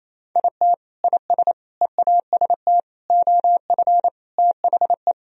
Morse code, heard as a keyed tone, about 28 wpm, 700 Hz. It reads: IT IS EAST OF THE
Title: Sentences from Top 1000 Words - Review Part 3 - 28wpm